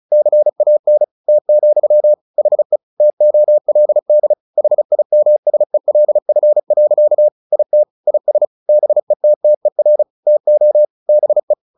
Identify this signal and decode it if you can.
Morse code, heard as a keyed tone, about 34 wpm, 600 Hz. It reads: CAN T, HE TOLD HIMSELF. IT IS BETTER TO BE
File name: the old man and the sea 2 35wpm